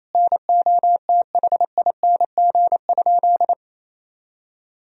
Morse code, heard as a keyed tone, about 28 wpm, 700 Hz. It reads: NOTHING?
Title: Sentences from Top 1000 Words - Review Part 4 - 28wpm